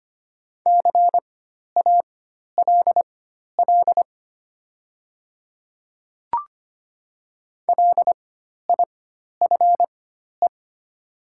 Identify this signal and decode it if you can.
Morse code, heard as a keyed tone, about 25 wpm, spaced as for 12 wpm, 700 Hz. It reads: CALL LIFE
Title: 4x Character Spacing - Top 300 Words - 25wpm